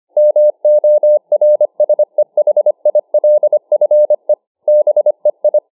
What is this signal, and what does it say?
Morse code, heard as a keyed tone, about 25 wpm, 600 Hz. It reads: MORSEHILFE BEI